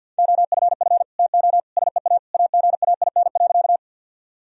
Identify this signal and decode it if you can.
Morse code, heard as a keyed tone, about 52 wpm, 700 Hz. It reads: OFF TO SEA AGAIN.
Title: Sentences from Top 1000 Words - Review Part 1 - 50wpm